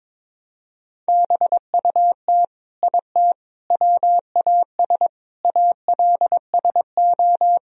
Morse code, heard as a keyed tone, about 22 wpm, 700 Hz. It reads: BUT IT WAS ALSO
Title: Sentences from Top 1000 Words - Review Part 2 - 22wpm